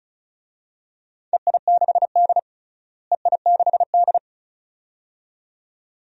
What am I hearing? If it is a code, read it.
Morse code, heard as a keyed tone, about 36 wpm, 700 Hz. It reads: EI6D EI6D
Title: DXCC-100 Call Signs - 4 Characters - Repeated 2x - 35wpm